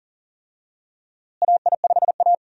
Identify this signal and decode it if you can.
Morse code, heard as a keyed tone, about 40 wpm, 700 Hz. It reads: AI5A
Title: USA Call Signs - 4 Characters - Repeated 2x - 40wpm